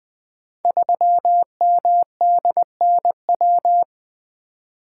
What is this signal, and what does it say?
Morse code, heard as a keyed tone, about 20 wpm, 700 Hz. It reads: 3MDNW
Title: Code Groups - 5 Letter:Numbers - 20wpm